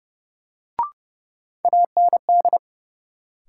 Morse code, heard as a keyed tone, about 30 wpm, 700 Hz. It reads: AND
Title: Sets of 2 Words - New Word Drill - 30wpm